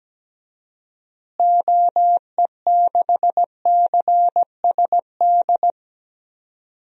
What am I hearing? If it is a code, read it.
Morse code, heard as a keyed tone, about 17 wpm, 700 Hz. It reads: OE6CSD
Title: DXCC-100 Call Signs - 6 Characters - Repeated 2x - 17wpm